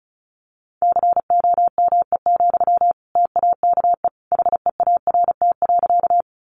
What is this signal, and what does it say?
Morse code, heard as a keyed tone, about 34 wpm, 700 Hz. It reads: COME, TAKE HEART.
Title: Sentences from Top 1200 Words 35wpm